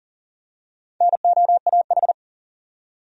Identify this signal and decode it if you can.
Morse code, heard as a keyed tone, about 39 wpm, 700 Hz. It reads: NOAH
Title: USA Common Names - Max 4 Letters - 40wpm